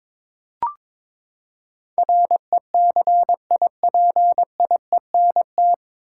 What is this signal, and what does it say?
Morse code, heard as a keyed tone, about 22 wpm, 700 Hz. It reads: RECIPIENT
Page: Morse Code Ninja practice 22wpm